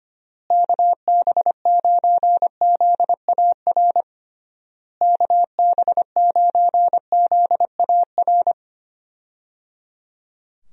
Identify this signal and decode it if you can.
Morse code, heard as a keyed tone, about 25 wpm, 700 Hz. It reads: KB9ZAR KB9ZAR
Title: USA Call Signs - 6 Characters - Repeated 3x - 25wpm